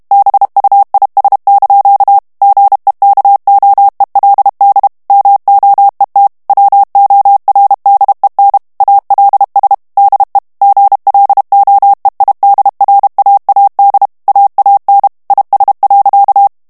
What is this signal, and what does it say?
Morse code, heard as a keyed tone, about 32 wpm, 800 Hz. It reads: BUIS) GEKOELD MOET WORDEN ALS DE GLOEIDRAAD AAN IS.
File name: HOLLANDAIS texte clair 2 vitesse 30 mots minute